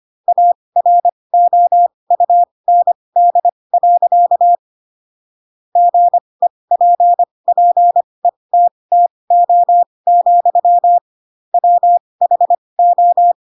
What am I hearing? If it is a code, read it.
Morse code, heard as a keyed tone, about 25 wpm, spaced as for 20 wpm, 700 Hz. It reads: AROUND. GEPPETTO, WHO
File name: Pinocchio / Pinocchio ch7 15 25 ik1hge